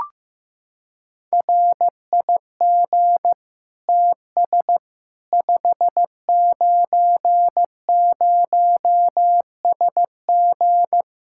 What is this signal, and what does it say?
Morse code, heard as a keyed tone, about 15 wpm, 700 Hz. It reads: RIG TS 590SG
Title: Practice QSO - Exchange 2 - Info - 15wpm